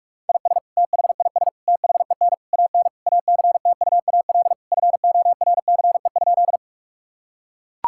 Morse code, heard as a keyed tone, about 41 wpm, 700 Hz. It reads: IS THIS THEN AN ACTUAL FORCE?
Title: Sentences from Top 1000 Words - Review Part 5 - 45wpm